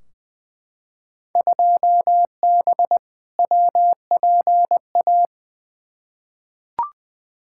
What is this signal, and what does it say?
Morse code, heard as a keyed tone, about 20 wpm, 700 Hz. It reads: 2B WPA
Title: Field Day Exchanges Encore - 20wpm